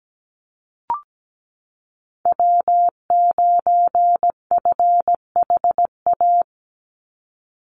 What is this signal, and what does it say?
Morse code, heard as a keyed tone, about 17 wpm, 700 Hz. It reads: W9FHA
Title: NA SOTA Chaser Callsigns 17wpm